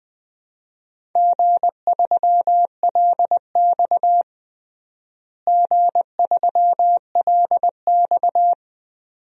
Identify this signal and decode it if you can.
Morse code, heard as a keyed tone, about 20 wpm, 700 Hz. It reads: G3LX G3LX
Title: DXCC-100 Call Signs - 4 Characters - Repeated 2x - 20wpm